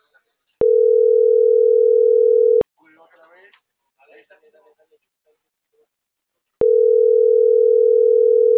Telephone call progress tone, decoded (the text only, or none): ringback tone